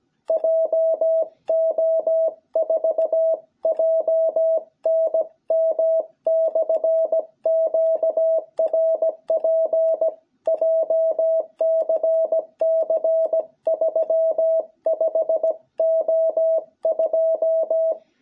Morse code, heard as a keyed tone, about 17 wpm, 660 Hz. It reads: JO4JNM/QRP JCC 35O2
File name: jh8cbh / 2021cw